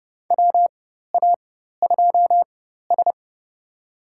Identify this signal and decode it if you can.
Morse code, heard as a keyed tone, about 30 wpm, spaced as for 14 wpm, 700 Hz. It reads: WA2S
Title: USA Call Signs - 4 Characters - 4x Character Spacing - 30wpm